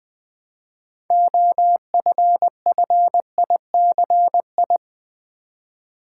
Morse code, heard as a keyed tone, about 20 wpm, 700 Hz. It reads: OFFICI
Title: Build A Word - 3 Letters At A Time - 8-Letter Words - 20wpm